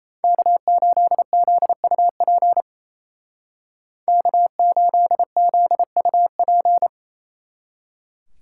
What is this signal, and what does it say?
Morse code, heard as a keyed tone, about 34 wpm, 700 Hz. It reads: K8ZUP K8ZUP
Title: Speed Racing - USA Call Signs 28wpm